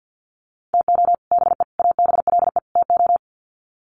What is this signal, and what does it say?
Morse code, heard as a keyed tone, about 48 wpm, 700 Hz. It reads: TO BE ABLE TO
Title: Sets of 4 Words 50wpm